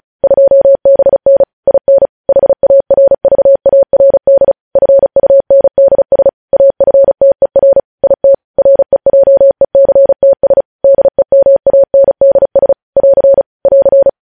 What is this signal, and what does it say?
Morse code, heard as a keyed tone, about 36 wpm, 550 Hz. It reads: 2BN IN HARVARD FUNDS AFTER IT REJECTS DEMANDS <AR> <AR>
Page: news headlines in morse code at 35 wpm